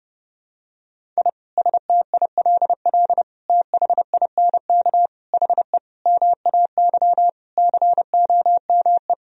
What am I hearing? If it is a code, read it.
Morse code, heard as a keyed tone, about 30 wpm, 700 Hz. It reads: I STILL THINK HE MAY COME
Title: Sentences from Top 1000 Words - Review Part 1 - 30wpm